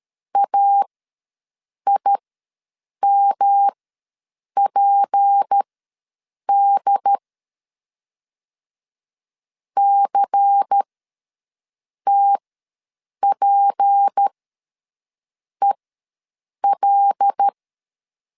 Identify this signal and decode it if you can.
Morse code, heard as a keyed tone, about 13 wpm, spaced as for 6 wpm, 800 Hz. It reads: AIMPD CTPEL
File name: LEZIONI / 6 PKX0 / 6 P ws 12wpm 3117